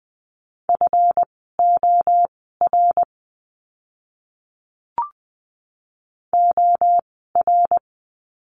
Morse code, heard as a keyed tone, about 20 wpm, spaced as for 14 wpm, 700 Hz. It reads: FOR OR
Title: 139 Lesson 23f - Letter f - Words 20wpm